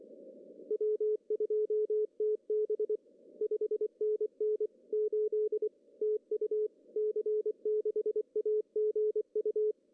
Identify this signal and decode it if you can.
Morse code, heard as a keyed tone, about 23 wpm, 410 Hz. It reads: W2TB 5NN 8 TU C6AGU
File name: CW / cqww-cw-k5tr